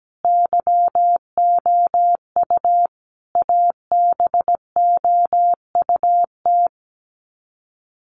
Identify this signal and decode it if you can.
Morse code, heard as a keyed tone, about 17 wpm, 700 Hz. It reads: YOU ABOUT
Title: Sets of 5 Words 17wpm